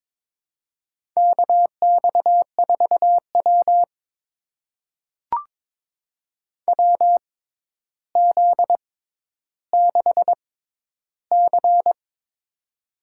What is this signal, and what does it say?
Morse code, heard as a keyed tone, about 22 wpm, spaced as for 11 wpm, 700 Hz. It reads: KX4W WZ6C
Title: USA Call Signs - 4 Characters - 6x Character Spacing - 22wpm